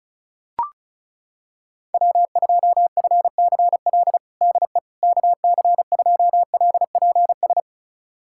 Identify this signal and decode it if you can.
Morse code, heard as a keyed tone, about 36 wpm, 700 Hz. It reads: W2FCL DE KC2LPS
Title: USA Callsign DE Callsign 35wpm